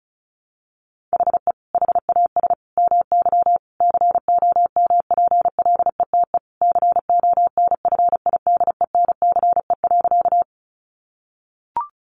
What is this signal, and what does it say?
Morse code, heard as a keyed tone, about 34 wpm, 700 Hz. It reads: HE HAS MY COMPLETE CONFIDENCE.
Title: Sentences from Top 1500 Words 35wpm